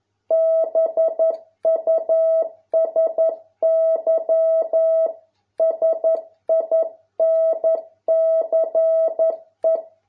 Morse code, heard as a keyed tone, about 11 wpm, 650 Hz. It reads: BUSY SINCE